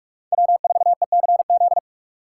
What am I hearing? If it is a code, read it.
Morse code, heard as a keyed tone, about 42 wpm, 700 Hz. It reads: W4ECZ